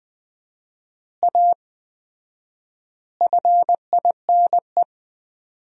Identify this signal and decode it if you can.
Morse code, heard as a keyed tone, about 20 wpm, 700 Hz. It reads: A FINE
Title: 4x Word-Spacing - Sentences from Top 500 Words - 20wpm